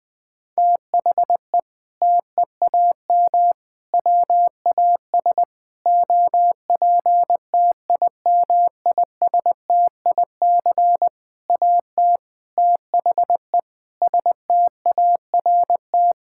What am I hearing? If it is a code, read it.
Morse code, heard as a keyed tone, about 20 wpm, 700 Hz. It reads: THE TEAM WAS OPTIMISTIC AT THE START
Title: Sentences with 6-Word Phrases - Spoken Beginning - 20wpm